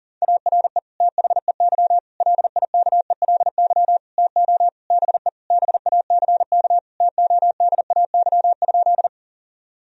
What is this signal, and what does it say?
Morse code, heard as a keyed tone, about 37 wpm, 700 Hz. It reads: ARE THEY LIKELY TO BE BACK TODAY?
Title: Sentences from Top 1000 Words - Review Part 4 - 40wpm